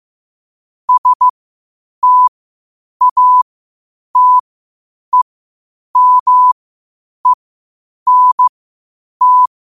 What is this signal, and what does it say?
Morse code, heard as a keyed tone, about 15 wpm, spaced as for 8 wpm, 1000 Hz. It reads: STATEMENT
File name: qotdfiles / 8wpm